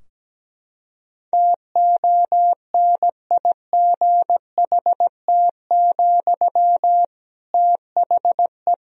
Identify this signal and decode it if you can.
Morse code, heard as a keyed tone, about 17 wpm, 700 Hz. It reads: TONIGHT, THE
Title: Sentences from Top 1000 Words - Review Part 4 - 17wpm